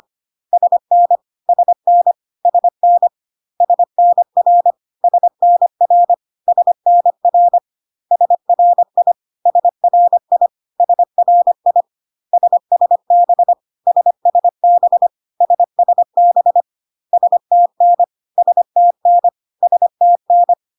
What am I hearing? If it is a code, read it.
Morse code, heard as a keyed tone, about 25 wpm, 700 Hz. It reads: SN SN SN SNR SNR SNR SRI SRI SRI SSB SSB SSB STN STN STN
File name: cw / CW common abbreviations alphabetical repeated 3 times 25 25wpm ik1hge